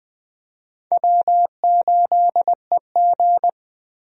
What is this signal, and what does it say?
Morse code, heard as a keyed tone, about 20 wpm, 700 Hz. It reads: W8EG
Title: USA Call Signs - 4 Characters - Repeated 3x - 20wpm